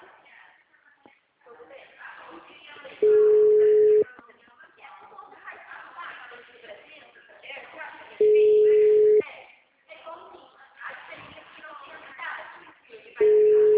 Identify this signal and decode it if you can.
Ringback tone (Europe, 425 Hz), with speech.
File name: out- -2045- - - .17846